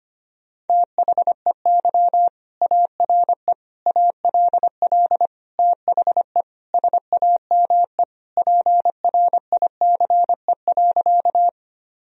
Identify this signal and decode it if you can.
Morse code, heard as a keyed tone, about 25 wpm, 700 Hz. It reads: THEY ARE ALL THE SAME PRICE.
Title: Sentences from Top 1000 Words - Review Part 4 - 25wpm